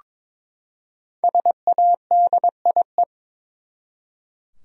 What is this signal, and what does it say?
Morse code, heard as a keyed tone, about 22 wpm, 700 Hz. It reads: SADIE
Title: USA Common Names - Max 5 Letters - 22wpm